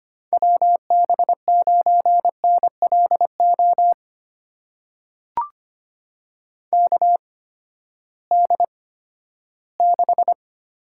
Morse code, heard as a keyed tone, about 25 wpm, spaced as for 14 wpm, 700 Hz. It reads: WB9NLO KD6
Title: USA Call Signs - 8x Character Spacing - 25wpm